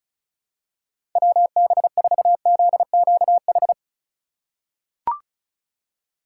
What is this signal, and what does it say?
Morse code, heard as a keyed tone, about 36 wpm, 700 Hz. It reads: WB4ZQH
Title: USA Call Signs - 6 Characters - 4x Character Spacing - 35wpm